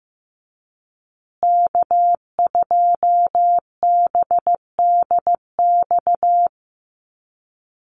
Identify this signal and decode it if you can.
Morse code, heard as a keyed tone, about 15 wpm, 700 Hz. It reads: K2BDX
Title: USA Call Signs Repeated 2x 15wpm